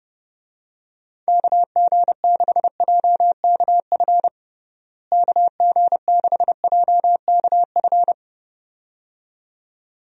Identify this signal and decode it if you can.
Morse code, heard as a keyed tone, about 31 wpm, 700 Hz. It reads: KG6JKF KG6JKF
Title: USA Call Signs - 6 Characters - Repeated 2x - 30wpm